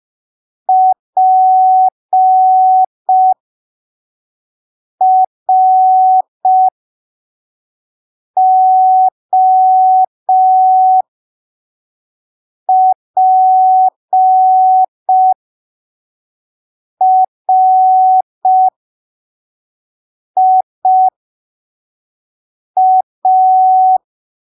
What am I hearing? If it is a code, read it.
Morse code, heard as a keyed tone, about 5 wpm, spaced as for 3 wpm, 750 Hz. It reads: PROPRIA
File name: arquivos / texto / texto 04 5ppm